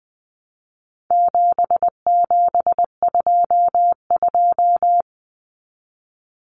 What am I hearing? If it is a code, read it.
Morse code, heard as a keyed tone, about 20 wpm, 700 Hz. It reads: 7722
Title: Numbers - 4-Digit 20wpm